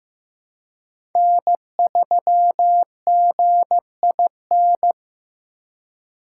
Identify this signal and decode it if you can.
Morse code, heard as a keyed tone, about 15 wpm, 700 Hz. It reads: N3GIN
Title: USA Call Signs - 5 Characters - 2x Character Spacing - 15wpm